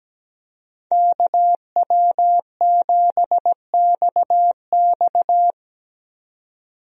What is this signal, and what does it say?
Morse code, heard as a keyed tone, about 17 wpm, 700 Hz. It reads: KW7XX
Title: USA Call Signs - 5 Characters - 17wpm